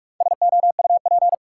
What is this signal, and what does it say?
Morse code, heard as a keyed tone, about 45 wpm, 700 Hz. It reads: SOUP
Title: Sentences from Top 1600 Words - New Word Drill Alt - 45wpm